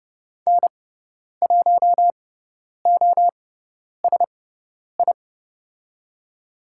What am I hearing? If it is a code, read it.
Morse code, heard as a keyed tone, about 30 wpm, spaced as for 10 wpm, 690 Hz. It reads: N1OSI
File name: Lesson 17g - Letter u - Call Signs 30wpm